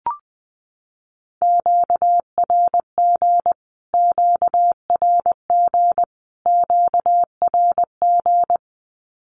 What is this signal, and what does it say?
Morse code, heard as a keyed tone, about 20 wpm, 700 Hz. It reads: QRG QRG QRG
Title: Q-Codes Repeated 3x 20wpm